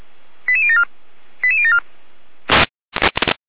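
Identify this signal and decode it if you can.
Five-tone selective call (ZVEI), heard as quick runs of five tones, tones of about 70 ms. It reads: 88074 88074
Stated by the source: Five-tone ZVEI selective call received with an SDR.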